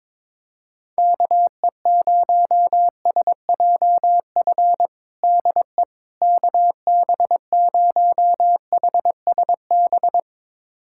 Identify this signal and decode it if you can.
Morse code, heard as a keyed tone, about 22 wpm, 700 Hz. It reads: KE0SJF DE KB0HSB
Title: USA Callsign DE Callsign - 6 Characters - 22wpm